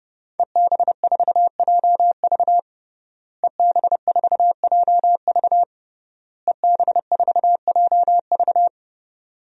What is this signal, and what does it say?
Morse code, heard as a keyed tone, about 30 wpm, 700 Hz. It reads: EB4JV EB4JV EB4JV
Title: DXCC-100 Call Signs - 5 Characters - Repeated 3x - 30wpm